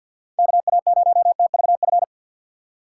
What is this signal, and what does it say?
Morse code, heard as a keyed tone, about 51 wpm, 700 Hz. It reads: KA0TVF
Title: USA Call Signs - 3x Character Spacing - 50wpm